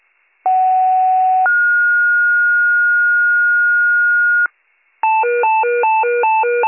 Two-tone page, heard as a one-second tone, then a three-second tone, about 750 then 1430 Hz. It is heard from 0.4 to 4.5 s.